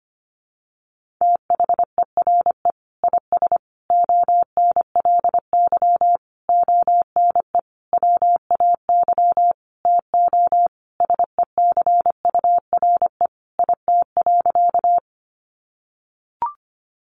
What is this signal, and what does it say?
Morse code, heard as a keyed tone, about 25 wpm, 700 Hz. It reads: THERE IS ONLY ONE WAY TO SECURE IT.
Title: Sentences from Top 1200 Words 25wpm